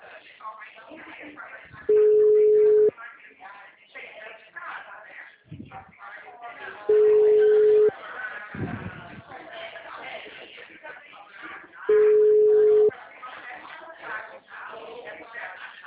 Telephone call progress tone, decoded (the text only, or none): ringback tone